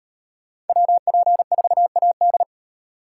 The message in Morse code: WP4AD

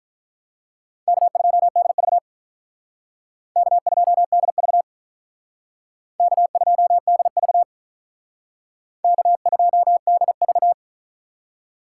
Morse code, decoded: K2DV K2DV K2DV K2DV

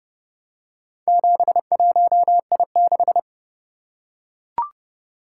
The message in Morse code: 71I6

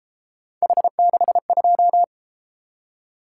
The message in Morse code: H62

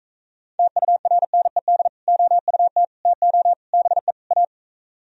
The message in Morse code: TURNED OUT TO BE A

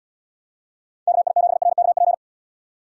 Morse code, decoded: KI7RKP

